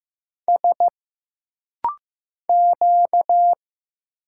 Morse code, heard as a keyed tone, about 15 wpm, 700 Hz. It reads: S Q
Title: Single Letters - Rapid-Fire 15wpm